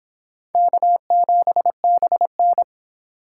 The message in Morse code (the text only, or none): K7BN